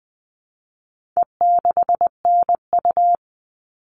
Morse code, heard as a keyed tone, about 20 wpm, 700 Hz. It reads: E6NU
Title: All Cut Numbers - 4-Digit 20wpm